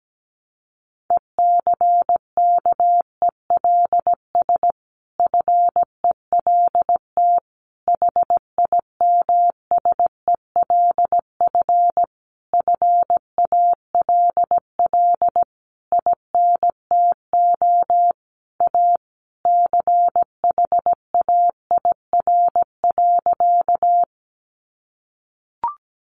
Morse code, heard as a keyed tone, about 17 wpm, 700 Hz. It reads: ECKELS FELT HIMSELF FALL INTO A CHAIR.
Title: A Sound Of Thunder 17wpm